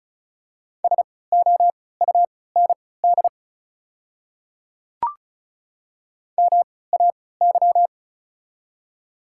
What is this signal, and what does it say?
Morse code, heard as a keyed tone, about 34 wpm, spaced as for 20 wpm, 700 Hz. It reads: SOUND MAY
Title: 3x Character Spacing - Top 100 Words - 35wpm